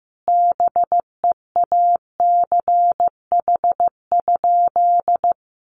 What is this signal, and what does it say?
Morse code, heard as a keyed tone, about 15 wpm, 700 Hz. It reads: BEACH?
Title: Sentences from Top 1500 Words 15wpm